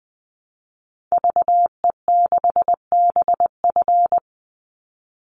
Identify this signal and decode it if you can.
Morse code, heard as a keyed tone, about 20 wpm, 700 Hz. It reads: VE6BF
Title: Call Signs Repeated 2x - 20wpm